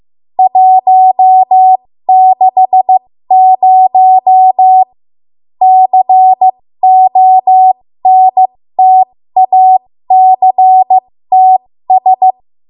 Morse code, heard as a keyed tone, about 15 wpm, spaced as for 13 wpm, 750 Hz. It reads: 160 CONTACTS
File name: Morse / 13WPM